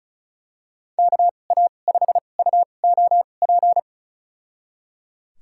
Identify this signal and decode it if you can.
Morse code, heard as a keyed tone, about 34 wpm, spaced as for 25 wpm, 700 Hz. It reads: KA5UOP
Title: USA Call Signs - 6 Characters - 2x Character Spacing - 35wpm